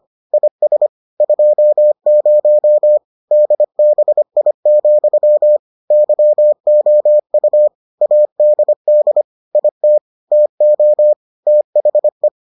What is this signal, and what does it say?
Morse code, heard as a keyed tone, about 25 wpm, 600 Hz. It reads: IS 20 DBI, YOU ADD IT TO THE